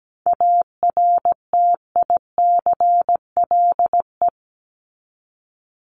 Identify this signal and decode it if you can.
Morse code, heard as a keyed tone, about 17 wpm, 700 Hz. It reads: ARTICLE
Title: Top 751-1000 Words 17wpm